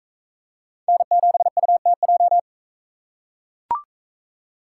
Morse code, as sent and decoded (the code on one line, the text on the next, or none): -. --... ..- - .---
N7UTJ